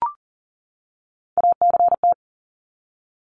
Morse code, heard as a keyed tone, about 40 wpm, 700 Hz. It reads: ACT
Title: Top 300 Words 40wpm